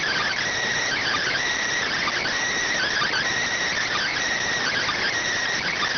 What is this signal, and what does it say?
Emergency Alert System (SAME) tones (header bursts), heard as harsh buzzing bursts. It is heard from 0.8 to 5.1 s.